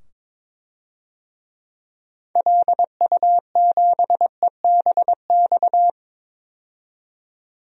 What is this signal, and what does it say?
Morse code, heard as a keyed tone, about 22 wpm, 700 Hz. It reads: LU7EBX
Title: DXCC-100 Call Signs - 6 Characters - 22wpm